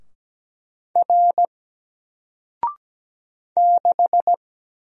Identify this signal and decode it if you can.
Morse code, heard as a keyed tone, about 17 wpm, 700 Hz. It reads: R 6
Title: Single Letter-Number Alt 17wpm